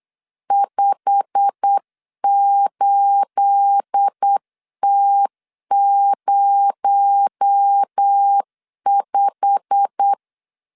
Morse code, heard as a keyed tone, about 8 wpm, 800 Hz. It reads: 58T05
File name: LEZIONI / 9 28 8wpm 3512